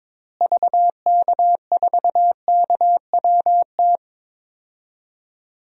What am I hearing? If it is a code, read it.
Morse code, heard as a keyed tone, about 22 wpm, 700 Hz. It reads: VK4KWT
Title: DXCC-100 Call Signs - 6 Characters - Repeated 2x - 22wpm